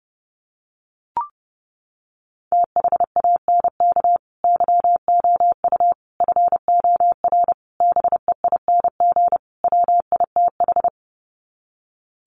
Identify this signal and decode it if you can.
Morse code, heard as a keyed tone, about 30 wpm, 700 Hz. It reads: THANK YOU FOR BEING WITH
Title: Sets of 5 Words 30wpm